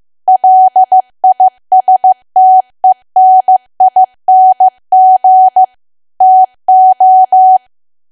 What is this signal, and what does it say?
Morse code, heard as a keyed tone, about 15 wpm, 750 Hz. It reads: LISTENING TO
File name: cwa / W1AW-Level2 / 15WPM